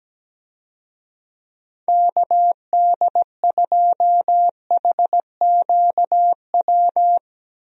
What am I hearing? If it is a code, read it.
Morse code, heard as a keyed tone, about 17 wpm, 700 Hz. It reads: KD2HQW
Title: USA Call Signs - 6 Characters - Repeated 2x - 17wpm